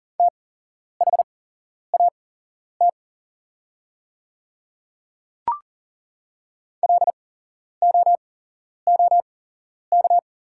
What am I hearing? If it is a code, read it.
Morse code, heard as a keyed tone, about 40 wpm, spaced as for 11 wpm, 690 Hz. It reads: THAT LOOK
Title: 8x Character Spacing - Top 200 Words - 40wpm